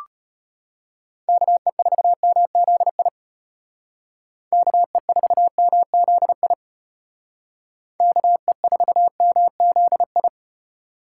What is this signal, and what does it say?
Morse code, heard as a keyed tone, about 33 wpm, 700 Hz. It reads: KE4MZI KE4MZI KE4MZI